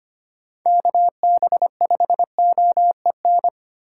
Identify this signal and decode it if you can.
Morse code, heard as a keyed tone, about 25 wpm, 700 Hz. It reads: KB5OEN
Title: USA Call Signs - 6 Characters - 3x Character Spacing - 25wpm